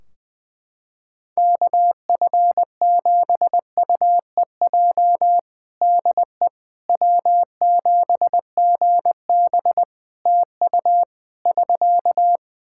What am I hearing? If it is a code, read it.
Morse code, heard as a keyed tone, about 20 wpm, 700 Hz. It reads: KF7UEJ DE W7GB TU <SK>
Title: Practice QSO - Exchange 4 - Ending - 20wpm